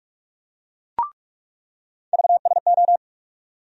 Morse code, heard as a keyed tone, about 42 wpm, 700 Hz. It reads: VSO
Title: USA Call Sign Postfixes - 3 Letters 45wpm